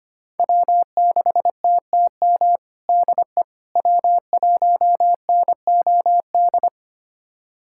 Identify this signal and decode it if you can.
Morse code, heard as a keyed tone, about 25 wpm, 700 Hz. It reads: W6TTM DE W1NOD
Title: USA Callsign DE Callsign - 5 Characters - 25wpm